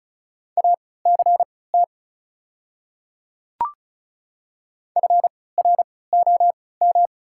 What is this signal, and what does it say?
Morse code, heard as a keyed tone, about 36 wpm, spaced as for 20 wpm, 700 Hz. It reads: ACT FROM
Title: 3x Character Spacing - Top 200 Words - 35wpm